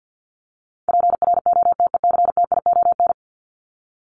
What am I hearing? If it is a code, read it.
Morse code, heard as a keyed tone, about 50 wpm, 700 Hz. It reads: PROTECTION
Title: Top 1501-1750 Words 50wpm